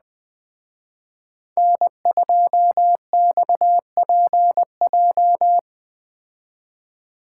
Morse code, heard as a keyed tone, about 20 wpm, 700 Hz. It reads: N2XPJ